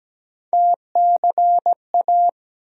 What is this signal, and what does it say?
Morse code, heard as a keyed tone, about 17 wpm, 700 Hz. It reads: TCA
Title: USA Call Sign Postfixes - 3 Letters 17wpm